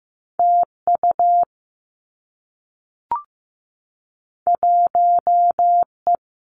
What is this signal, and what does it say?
Morse code, heard as a keyed tone, about 15 wpm, 700 Hz. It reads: T U 1 E
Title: Two Challenging Letter-Number Combinations 15wpm